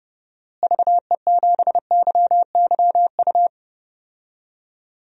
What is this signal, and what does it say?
Morse code, heard as a keyed tone, about 31 wpm, 700 Hz. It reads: VE7YYU
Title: DXCC-100 Call Signs - 6 Characters - Repeated 2x - 30wpm